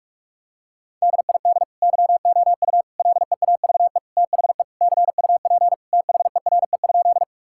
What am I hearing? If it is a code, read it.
Morse code, heard as a keyed tone, about 46 wpm, 700 Hz. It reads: DID YOU LEAVE THE CUP THERE?
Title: Sentences from Top 1600 Words 45wpm